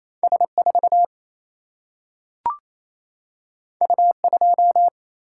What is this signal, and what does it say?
Morse code, heard as a keyed tone, about 28 wpm, 700 Hz. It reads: S4 U2
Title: Single-Letter Single-Number 28wpm